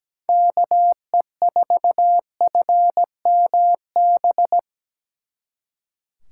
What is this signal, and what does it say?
Morse code, heard as a keyed tone, about 20 wpm, 700 Hz. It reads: KE4FMB
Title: Speed Racing - USA 2x3 Call Signs 17wpm